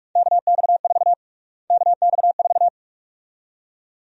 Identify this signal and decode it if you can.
Morse code, heard as a keyed tone, about 46 wpm, 700 Hz. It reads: KX4 KX4
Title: USA Call Sign Prefixes - Repeated 2x - 45wpm